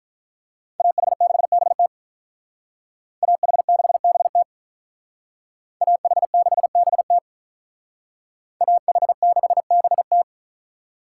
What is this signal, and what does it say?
Morse code, heard as a keyed tone, about 42 wpm, 700 Hz. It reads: AH6BT AH6BT AH6BT AH6BT
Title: Speed Racing - USA 2x2 Call Signs 35wpm